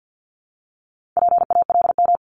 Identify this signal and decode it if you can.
Morse code, heard as a keyed tone, about 50 wpm, 700 Hz. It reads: PALM
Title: Multi-letter Phonemes Repeated 3x 50wpm